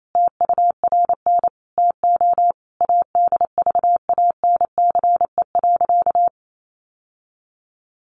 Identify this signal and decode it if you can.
Morse code, heard as a keyed tone, about 28 wpm, 700 Hz. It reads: TURN TO ADVANCE.
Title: Sentences from Top 1400 Words 28wpm